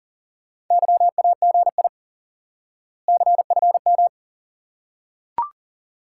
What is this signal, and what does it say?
Morse code, heard as a keyed tone, about 40 wpm, 700 Hz. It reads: YAGI CFM
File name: IQE Course 3 - Lesson 89 - YAGI - Two Elements - 40wpm